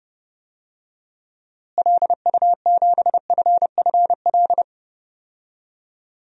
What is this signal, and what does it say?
Morse code, heard as a keyed tone, about 30 wpm, 700 Hz. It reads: LU7FFL
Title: DXCC-100 Call Signs - 6 Characters - Repeated 3x - 30wpm